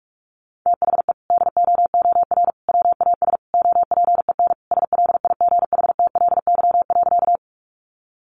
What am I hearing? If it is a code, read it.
Morse code, heard as a keyed tone, about 44 wpm, 700 Hz. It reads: THE DOOR WAS OPEN SLIGHTLY.
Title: Sentences from Top 1300 Words 45wpm